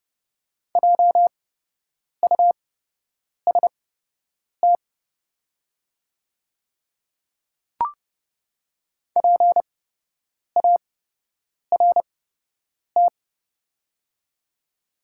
Morse code, heard as a keyed tone, about 30 wpm, spaced as for 8 wpm, 700 Hz. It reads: JUST PART
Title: 8x Character Spacing - Top 300 Words - 30wpm